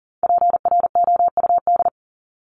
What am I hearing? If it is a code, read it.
Morse code, heard as a keyed tone, about 40 wpm, 700 Hz. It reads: PROUD
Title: Top 1501-1750 Words 40wpm